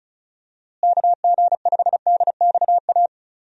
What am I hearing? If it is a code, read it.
Morse code, heard as a keyed tone, about 34 wpm, 700 Hz. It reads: KG5DXA
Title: USA Call Signs - 6 Characters - Repeated 2x - 35wpm